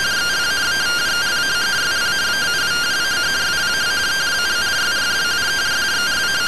Two-tone teleprinter signal, about 170 Hz shift.